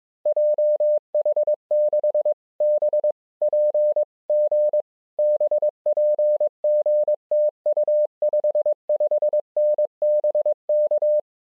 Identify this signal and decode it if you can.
Morse code, heard as a keyed tone, about 22 wpm, 600 Hz. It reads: JH6BPG BPG TU 55N BK